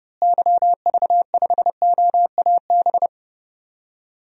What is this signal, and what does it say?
Morse code, heard as a keyed tone, about 30 wpm, 700 Hz. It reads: YV5OAB
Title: DXCC-100 Call Signs - 6 Characters - Repeated 3x - 30wpm